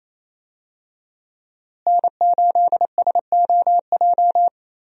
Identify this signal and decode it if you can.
Morse code, heard as a keyed tone, about 28 wpm, 700 Hz. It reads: N8SOJ
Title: USA Call Signs - 5 Characters - Repeated 2x - 28wpm